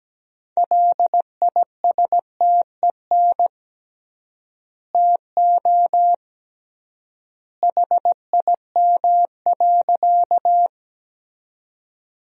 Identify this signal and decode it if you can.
Morse code, heard as a keyed tone, about 17 wpm, 700 Hz. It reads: LISTEN TO HIM.
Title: 3x Word-Spacing - Sentences from Top 400 Words - 17wpm